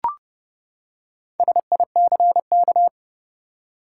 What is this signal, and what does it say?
Morse code, heard as a keyed tone, about 31 wpm, 700 Hz. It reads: SICK
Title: Four Letter Words - 30wpm